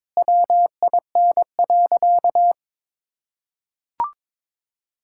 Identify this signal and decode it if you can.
Morse code, heard as a keyed tone, about 21 wpm, 700 Hz. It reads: WIN.
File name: practice / 22wpm / Sentences with 3-Word Phrases - 22wpm